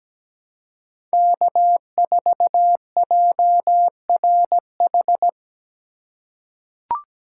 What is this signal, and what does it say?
Morse code, heard as a keyed tone, about 17 wpm, 700 Hz. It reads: K4JRH
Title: USA Call Signs - 5 Characters - 2x Character Spacing - 17wpm